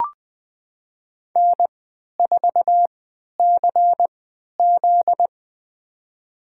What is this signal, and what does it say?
Morse code, heard as a keyed tone, about 20 wpm, spaced as for 11 wpm, 700 Hz. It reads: N4CZ
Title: USA Call Signs - 4 Characters - 3x Character Spacing - 20wpm